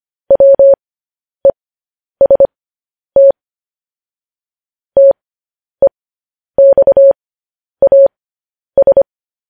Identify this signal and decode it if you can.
Morse code, heard as a keyed tone, about 25 wpm, spaced as for 10 wpm, 550 Hz. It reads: WEST TEXAS
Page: news headlines in morse code at 10 wpm